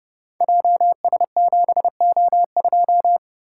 Morse code, heard as a keyed tone, about 30 wpm, 700 Hz. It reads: JS7O2
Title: Code Groups - 5 Letter:Numbers - 30wpm